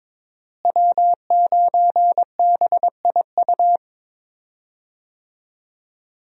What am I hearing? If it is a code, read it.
Morse code, heard as a keyed tone, about 22 wpm, 700 Hz. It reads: W9BIU
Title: USA Call Signs - 5 Characters - Repeated 3x - 22wpm